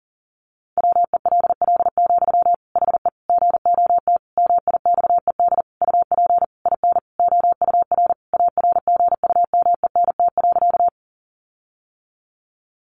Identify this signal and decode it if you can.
Morse code, heard as a keyed tone, about 40 wpm, 700 Hz. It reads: WELL, HE GOT MIXED UP IN OUR ARGUMENT.
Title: Sentences from Top 1500 Words 40wpm